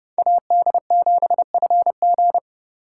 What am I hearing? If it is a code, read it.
Morse code, heard as a keyed tone, about 30 wpm, 700 Hz. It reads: AD7FG
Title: USA Call Signs - 3x Character Spacing - 30wpm